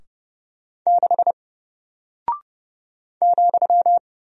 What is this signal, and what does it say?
Morse code, heard as a keyed tone, about 30 wpm, 700 Hz. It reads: 6 ,
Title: Single Character (Letter:Number:Punctuation:Prosign) Alt 30wpm